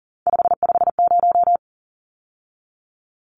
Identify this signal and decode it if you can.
Morse code, heard as a keyed tone, about 40 wpm, 700 Hz. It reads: 550
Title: Numbers - 3-Digit 40wpm